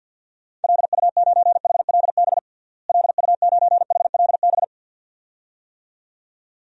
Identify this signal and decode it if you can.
Morse code, heard as a keyed tone, about 53 wpm, 700 Hz. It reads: LU9HLB LU9HLB
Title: DXCC-100 Call Signs - 6 Characters - Repeated 2x - 50wpm